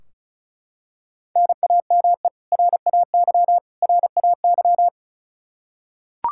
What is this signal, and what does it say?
Morse code, heard as a keyed tone, about 34 wpm, 700 Hz. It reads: NAME RAY RAY
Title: Practice QSO - Exchange 1 - Beginning - 35wpm